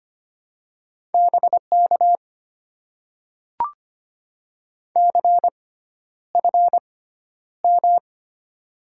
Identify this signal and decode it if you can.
Morse code, heard as a keyed tone, about 25 wpm, spaced as for 11 wpm, 700 Hz. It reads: BK CFM
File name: Top 90 QSO Elements - 6x Character Spacing - 25wpm